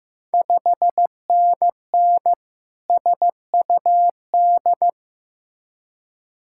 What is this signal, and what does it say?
Morse code, heard as a keyed tone, about 16 wpm, 700 Hz. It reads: 5NN SUD
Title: Ontario QSO Party Exchange 15wpm